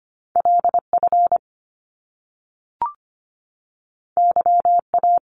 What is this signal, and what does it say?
Morse code, heard as a keyed tone, about 25 wpm, 700 Hz. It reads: LF YA
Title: Top Two Letter Combinations in Words 25wpm